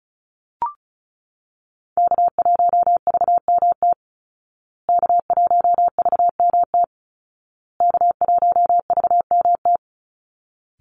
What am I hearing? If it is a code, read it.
Morse code, heard as a keyed tone, about 34 wpm, 700 Hz. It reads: K1VMT K1VMT K1VMT
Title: Call Signs Repeated 2x - 35wpm